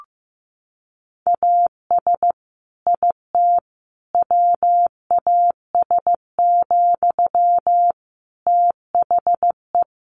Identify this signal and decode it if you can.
Morse code, heard as a keyed tone, about 15 wpm, 700 Hz. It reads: AS IT WAS, THE
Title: Sentences from Top 1500 Words 15wpm